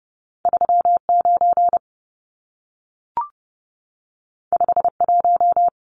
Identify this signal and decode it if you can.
Morse code, heard as a keyed tone, about 30 wpm, 700 Hz. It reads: 39 51
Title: Numbers - 2-Digit 30wpm